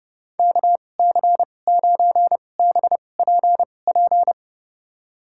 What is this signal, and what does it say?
Morse code, heard as a keyed tone, about 30 wpm, spaced as for 22 wpm, 700 Hz. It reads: KC9BPP
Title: USA Call Signs - 6 Characters - 2x Character Spacing - 30wpm